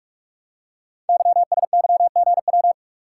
Morse code, heard as a keyed tone, about 48 wpm, 700 Hz. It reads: YSYGW
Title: Code Groups - 5 Letters - 45wpm